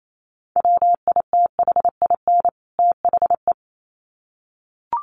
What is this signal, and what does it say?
Morse code, heard as a keyed tone, about 28 wpm, 700 Hz. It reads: WITHIN THE
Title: Sets of 2 Words 28wpm